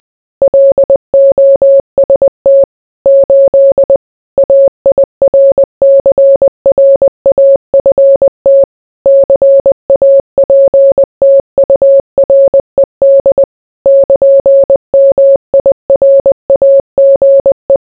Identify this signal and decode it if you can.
Morse code, heard as a keyed tone, about 20 wpm, 550 Hz. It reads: LOST 8 AIRCRAFT CAPTURED <KN>MIRAGE